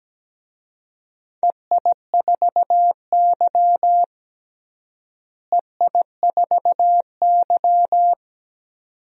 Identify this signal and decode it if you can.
Morse code, heard as a keyed tone, about 17 wpm, 700 Hz. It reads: EI4Y EI4Y
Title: DXCC-100 Call Signs - 4 Characters - Repeated 2x - 17wpm